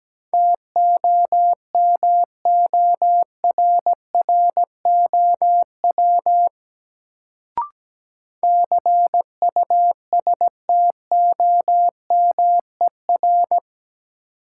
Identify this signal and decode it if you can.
Morse code, heard as a keyed tone, about 17 wpm, 700 Hz. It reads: TOMORROW CUSTOMER
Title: Sentences from Top 1100 Words - New Word Drill Alt - 17wpm